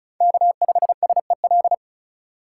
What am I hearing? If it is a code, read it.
Morse code, heard as a keyed tone, about 36 wpm, 700 Hz. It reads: K5SEL
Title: USA Call Signs - 5 Characters - 3x Character Spacing - 35wpm